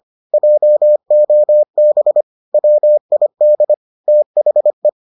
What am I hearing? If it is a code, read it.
Morse code, heard as a keyed tone, about 25 wpm, 600 Hz. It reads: JOB WID THE
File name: 25wpm-6